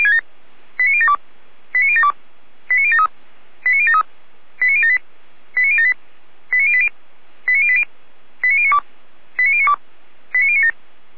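Five-tone selective call (ZVEI), heard as quick runs of five tones, tones of about 70 ms. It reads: 89072 89072 89073 89073 89078 89078 89080 89080 89092 89092 89097
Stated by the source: ZVEI five-tone call received by SDR.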